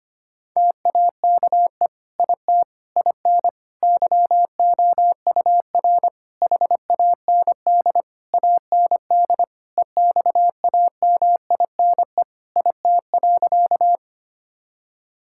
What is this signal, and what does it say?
Morse code, heard as a keyed tone, about 25 wpm, 700 Hz. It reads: TAKE IT IN YOUR HAND AND EXAMINE IT.
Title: Sentences from Top 1600 Words 25wpm